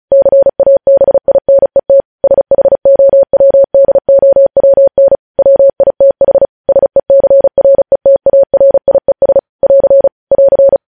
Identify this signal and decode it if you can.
Morse code, heard as a keyed tone, about 36 wpm, 550 Hz. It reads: CABINET SHOWDOWN WITH SECRETARIES <AR> <AR>
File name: podcast / News headlines-1x-35x35